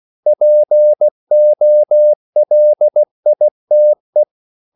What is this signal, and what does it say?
Morse code, heard as a keyed tone, about 16 wpm, 600 Hz. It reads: POLITE